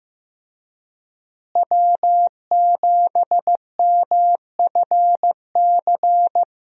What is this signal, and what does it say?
Morse code, heard as a keyed tone, about 16 wpm, 700 Hz. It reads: W7MFC
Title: USA Call Signs - 5 Characters - Repeated 3x - 15wpm